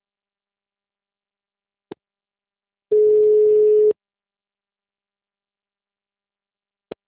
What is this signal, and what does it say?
Ringback tone (Europe, 425 Hz).